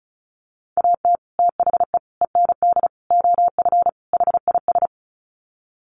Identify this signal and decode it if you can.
Morse code, heard as a keyed tone, about 34 wpm, 700 Hz. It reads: AT THE END OF HIS
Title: Sets of 5 Words Encore 35wpm